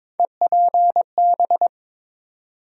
Morse code, heard as a keyed tone, about 22 wpm, 700 Hz. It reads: EPB
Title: USA Call Sign Postfixes - 3 Letters 22wpm